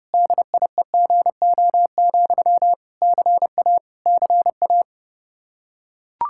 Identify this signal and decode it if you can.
Morse code, heard as a keyed tone, about 32 wpm, 700 Hz. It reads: DIEGO, CA CA
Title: Practice QSO - Exchange 1 - Beginning - 30wpm